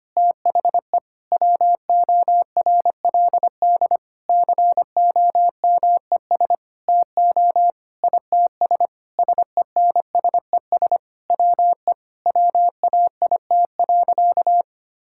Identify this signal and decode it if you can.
Morse code, heard as a keyed tone, about 25 wpm, 700 Hz. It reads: THE WORLD COMES TO ITS SENSES WE WAIT.
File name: practice / 25wpm / Sentences with 9-Word Phrases - 25wpm